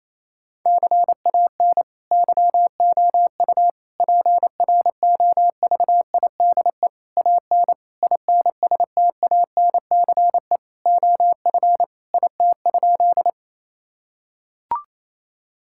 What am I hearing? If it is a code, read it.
Morse code, heard as a keyed tone, about 28 wpm, 700 Hz. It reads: CAN YOU PROVIDE AN INSTANCE OF IT?
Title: Sentences from Top 1000 Words - Review Part 5 - 28wpm